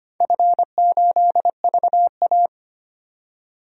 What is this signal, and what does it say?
Morse code, heard as a keyed tone, about 25 wpm, 700 Hz. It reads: F8VA
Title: DXCC-100 Call Signs - 4 Characters - 25wpm